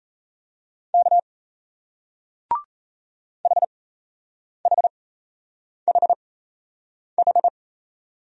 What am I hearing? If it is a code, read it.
Morse code, heard as a keyed tone, about 38 wpm, spaced as for 8 wpm, 700 Hz. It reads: K HHHH